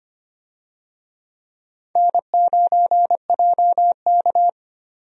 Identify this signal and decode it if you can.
Morse code, heard as a keyed tone, about 25 wpm, 700 Hz. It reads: N9JK
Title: USA Call Signs - 4 Characters - 25wpm